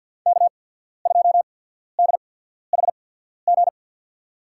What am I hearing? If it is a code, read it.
Morse code, heard as a keyed tone, about 53 wpm, spaced as for 14 wpm, 700 Hz. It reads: K2DHG
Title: USA Call Signs - 5 Characters - 8x Character Spacing - 50wpm